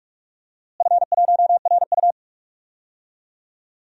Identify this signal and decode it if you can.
Morse code, heard as a keyed tone, about 42 wpm, 700 Hz. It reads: F1RU
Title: DXCC-100 Call Signs - 4 Characters - Repeated 3x - 45wpm